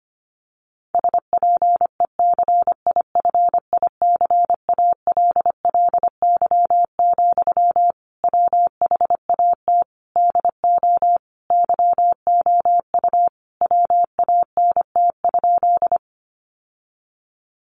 Morse code, heard as a keyed tone, about 25 wpm, 700 Hz. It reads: SPECIFICALLY, WHAT DO YOU WANT?
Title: Sentences from Top 1100 Words 25wpm